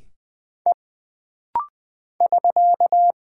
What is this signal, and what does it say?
Morse code, heard as a keyed tone, about 20 wpm, 700 Hz. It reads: E <SK>
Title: Single Character (Letter:Number:Punctuation:Prosign) - Rapid-Fire - 20wpm